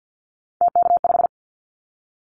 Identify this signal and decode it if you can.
Morse code, heard as a keyed tone, about 50 wpm, 700 Hz. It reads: TK5
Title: Callsign Prefixes for Top 100 DXCC - 50wpm